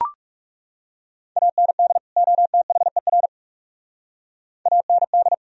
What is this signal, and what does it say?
Morse code, heard as a keyed tone, about 39 wpm, 700 Hz. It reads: AND OTHER AND